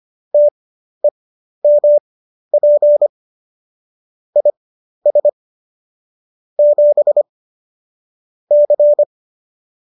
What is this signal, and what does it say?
Morse code, heard as a keyed tone, about 25 wpm, spaced as for 12 wpm, 590 Hz. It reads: TEMP IS 7 C